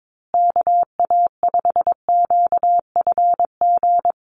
Morse code, heard as a keyed tone, about 22 wpm, 700 Hz. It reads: KA5QFG
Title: USA Call Signs Repeated 3x 22wpm